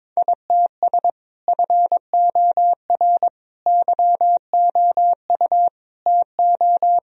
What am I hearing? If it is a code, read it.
Morse code, heard as a keyed tone, about 22 wpm, 700 Hz. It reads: ITS FOR YOU TO
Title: Sentences from Top 1000 Words - Review Part 3 - 22wpm